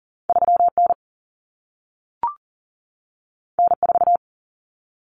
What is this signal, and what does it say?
Morse code, heard as a keyed tone, about 40 wpm, 700 Hz. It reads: 3N N4
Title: Common Cut Numbers - 2-Digit 40wpm